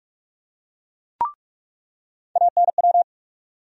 Morse code, heard as a keyed tone, about 44 wpm, 700 Hz. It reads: ANW
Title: USA Call Sign Postfixes - 3 Letters 45wpm